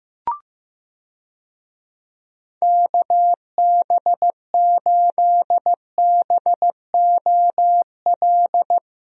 Morse code, heard as a keyed tone, about 15 wpm, 700 Hz. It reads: KB8BOL
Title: USA Call Signs - 6 Characters - Repeated 2x - 15wpm